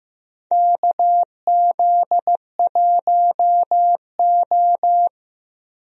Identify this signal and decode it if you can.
Morse code, heard as a keyed tone, about 15 wpm, 700 Hz. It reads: KZ1O
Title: USA Call Signs - 4 Characters - 15wpm